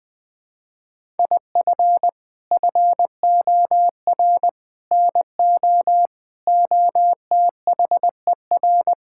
Morse code, heard as a keyed tone, about 20 wpm, 700 Hz. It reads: IF FOR NO OTHER
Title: Sets of 6 Words - 20wpm